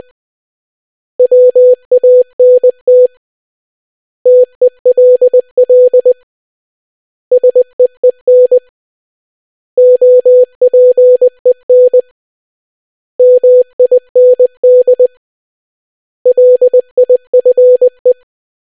Morse code, heard as a keyed tone, about 20 wpm, 500 Hz. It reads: WANT TELL SEEN OPEN MIND LIFE